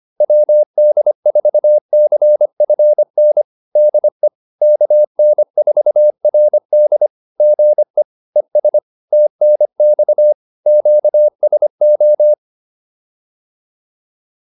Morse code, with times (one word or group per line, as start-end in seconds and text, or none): 0.2-3.4 WD4CFN
3.8-4.3 DE
4.6-7.1 KN4RD
7.4-8.0 GE
8.4-8.8 ES
9.1-10.3 TNX
10.7-12.3 QSO